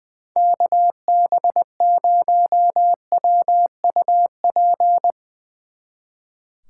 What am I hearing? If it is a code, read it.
Morse code, heard as a keyed tone, about 20 wpm, 700 Hz. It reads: KB0WUP
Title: Speed Racing - USA 2x3 Call Signs 20wpm